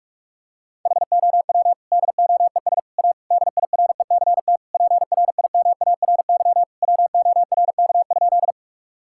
Morse code, heard as a keyed tone, about 46 wpm, 690 Hz. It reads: HOW DOES A DIRECT PRIMARY WORK?
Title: Sentences from Top 1000 Words - Review Part 5 - 45wpm